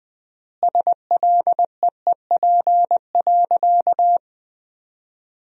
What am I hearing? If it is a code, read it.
Morse code, heard as a keyed tone, about 20 wpm, 700 Hz. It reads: SLEEP.
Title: 3x Word-Spacing - Sentences from Top 500 Words - 20wpm